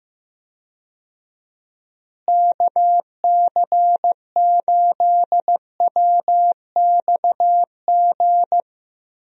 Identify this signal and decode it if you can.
Morse code, heard as a keyed tone, about 15 wpm, 700 Hz. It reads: KC8WXG
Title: USA Call Signs - 6 Characters - Repeated 2x - 15wpm